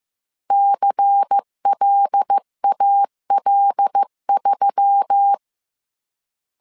Morse code, heard as a keyed tone, about 15 wpm, 800 Hz. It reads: CLAL3